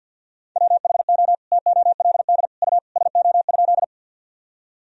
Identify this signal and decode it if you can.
Morse code, heard as a keyed tone, about 50 wpm, 700 Hz. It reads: WHO TOLD U SO?
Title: Sentences from Top 400 Words and Abbreviations 50wpm